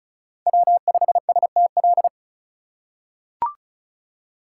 Morse code, heard as a keyed tone, about 34 wpm, 700 Hz. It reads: W5STL
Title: USA Call Signs - 5 Characters - 2x Character Spacing - 35wpm